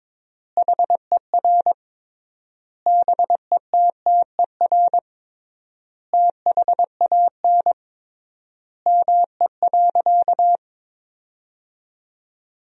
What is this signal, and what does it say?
Morse code, heard as a keyed tone, about 22 wpm, 700 Hz. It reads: HER BETTER THAN ME.
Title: 3x Word-Spacing - Sentences from Top 400 Words - 22wpm